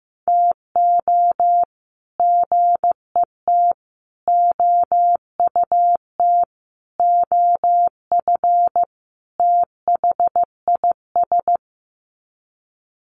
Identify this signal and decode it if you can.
Morse code, heard as a keyed tone, about 15 wpm, 700 Hz. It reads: TO GET OUT OF THIS
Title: Sets of 5 Words Encore2 15wpm